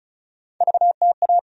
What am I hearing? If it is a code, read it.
Morse code, heard as a keyed tone, about 36 wpm, 700 Hz. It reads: VTA
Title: USA Call Sign Postfixes - 3 Letters 35wpm